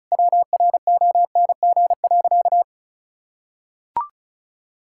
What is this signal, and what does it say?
Morse code, heard as a keyed tone, about 36 wpm, 700 Hz. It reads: WRONG.
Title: Sentences from Top 1000 Words - Review Part 5 - 35wpm